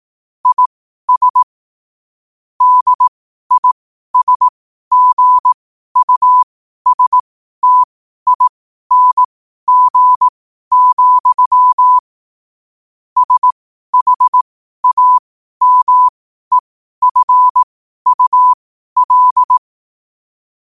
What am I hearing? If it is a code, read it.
Morse code, heard as a keyed tone, about 18 wpm, spaced as for 12 wpm, 1000 Hz. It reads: IS DISGUSTING, SHAMEFUL
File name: qotdfiles / 12wpm